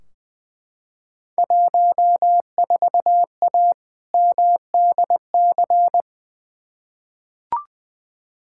Morse code, heard as a keyed tone, about 20 wpm, 700 Hz. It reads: 14A MDC
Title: Field Day Exchanges Encore - 20wpm